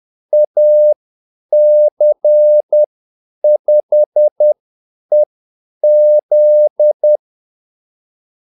Morse code, heard as a keyed tone, about 10 wpm, spaced as for 8 wpm, 600 Hz. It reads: AC5EZ
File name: Callsigns-10-8wpm